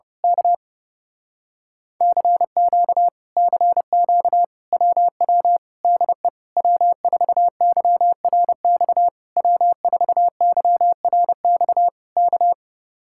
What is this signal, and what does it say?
Morse code, heard as a keyed tone, about 28 wpm, 700 Hz. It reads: K CQ CQ WW DE W4YRX W4YRX K
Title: Speed-Racing Common QSO Phrases - 30wpm